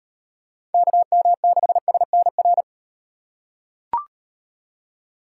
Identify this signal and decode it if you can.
Morse code, heard as a keyed tone, about 38 wpm, 700 Hz. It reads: KM6SNR